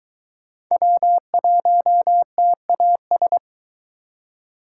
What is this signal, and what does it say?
Morse code, heard as a keyed tone, about 22 wpm, 700 Hz. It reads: W1TAS